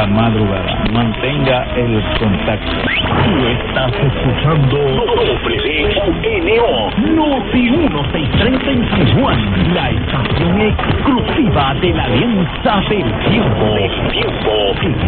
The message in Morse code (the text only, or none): none